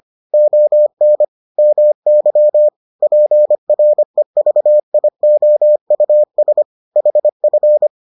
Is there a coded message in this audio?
Yes: Morse code from 0.3 to 7.9 s, heard as a keyed tone, about 25 wpm, 600 Hz. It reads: ON MY PREVIOUS HF